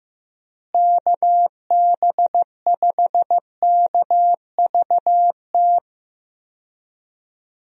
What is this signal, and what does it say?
Morse code, heard as a keyed tone, about 15 wpm, 700 Hz. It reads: KB5KVT
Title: USA Call Signs - 6 Characters - Repeated 3x - 15wpm